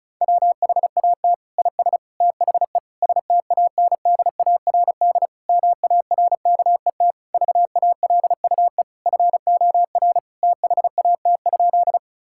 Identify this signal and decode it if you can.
Morse code, heard as a keyed tone, about 34 wpm, 700 Hz. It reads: WHAT IS THE STANDARD MARKET VALUE FOR THAT?
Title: Sentences from Top 1000 Words - Review Part 6 - 35wpm